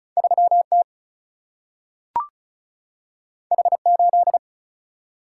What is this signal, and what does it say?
Morse code, heard as a keyed tone, about 36 wpm, 700 Hz. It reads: 3T H8
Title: Single-Letter Single-Number 35wpm